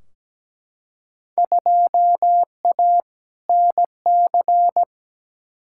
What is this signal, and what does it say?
Morse code, heard as a keyed tone, about 17 wpm, 700 Hz. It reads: 2A NC
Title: Field Day Exchanges - 17wpm